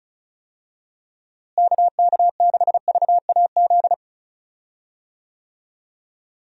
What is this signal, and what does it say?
Morse code, heard as a keyed tone, about 36 wpm, 700 Hz. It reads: KK6VAZ
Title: USA Call Signs - 6 Characters - Repeated 3x - 35wpm